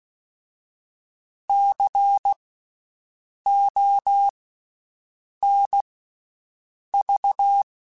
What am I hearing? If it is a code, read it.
Morse code, heard as a keyed tone, about 16 wpm, 800 Hz. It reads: C O N V
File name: Special INGs-mp3s / Special INGs-mp3s / Slow ing @15